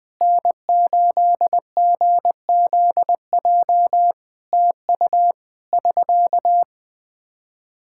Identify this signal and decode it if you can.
Morse code, heard as a keyed tone, about 21 wpm, 700 Hz. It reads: N8GZJ TU <SK>
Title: K1USN SST QSO Practice 20wpm